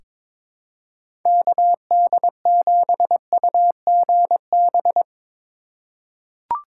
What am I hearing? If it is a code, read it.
Morse code, heard as a keyed tone, about 22 wpm, 700 Hz. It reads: KD7UGB
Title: USA Call Signs - 6 Characters - 4x Character Spacing - 22wpm